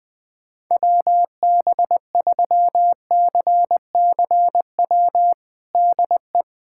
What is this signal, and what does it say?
Morse code, heard as a keyed tone, about 20 wpm, 700 Hz. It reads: WB3CCW DE
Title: USA Callsign DE Callsign - 6 Characters - 20wpm